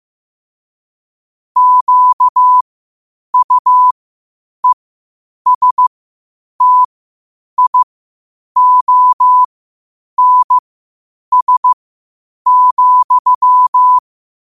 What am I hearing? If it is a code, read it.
Morse code, heard as a keyed tone, about 15 wpm, spaced as for 8 wpm, 1000 Hz. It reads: QUESTIONS,
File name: qotdfiles / 8wpm